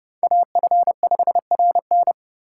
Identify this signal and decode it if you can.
Morse code, heard as a keyed tone, about 38 wpm, 700 Hz. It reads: AF5RN